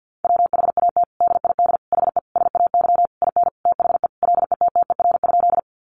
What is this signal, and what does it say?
Morse code, heard as a keyed tone, about 50 wpm, 700 Hz. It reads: WHAT DID HE SAY IN THE LETTER?
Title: Sentences from Top 300 Words 50wpm